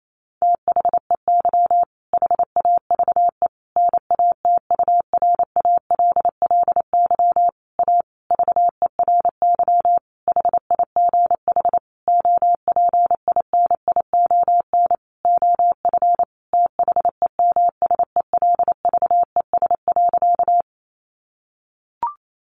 Morse code, heard as a keyed tone, about 28 wpm, 700 Hz. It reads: THEY HAVE NATURALLY A VERY HIGH OPINION OF THEMSELVES.
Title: Sentences from Top 1300 Words 28wpm